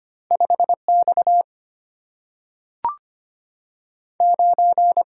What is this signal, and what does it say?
Morse code, heard as a keyed tone, about 25 wpm, 700 Hz. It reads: 5X 9
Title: Single-Letter Single-Number 25wpm